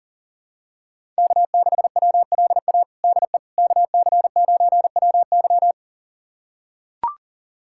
Morse code, heard as a keyed tone, about 40 wpm, 700 Hz. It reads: K6WLA DE KC9WY
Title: USA Callsign DE Callsign 40wpm